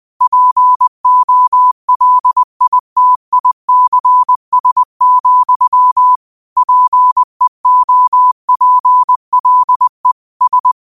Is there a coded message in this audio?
Yes: Morse code from 0.2 to 10.7 s, heard as a keyed tone, about 20 wpm, 1000 Hz. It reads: POLITICS, PEOPLE S